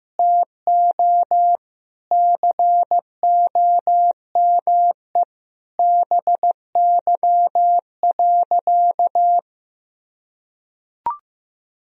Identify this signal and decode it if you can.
Morse code, heard as a keyed tone, about 15 wpm, 700 Hz. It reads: TO COME BY.
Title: Sentences from Top 1000 Words - Review Part 3 - 15wpm